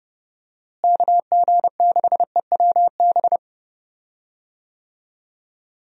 Morse code, heard as a keyed tone, about 31 wpm, 700 Hz. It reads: KG6EWB
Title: USA Call Signs - 6 Characters - Repeated 3x - 30wpm